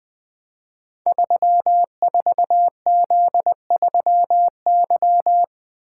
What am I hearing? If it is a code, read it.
Morse code, heard as a keyed tone, about 20 wpm, 700 Hz. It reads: 34Z3Y
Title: Code Groups - 5 Letter:Numbers - 20wpm